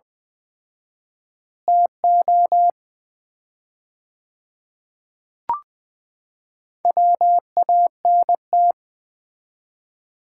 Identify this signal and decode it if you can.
Morse code, heard as a keyed tone, about 20 wpm, 700 Hz. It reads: TO WANT
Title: 4x Word-Spacing - Sets of 5 Words - 20wpm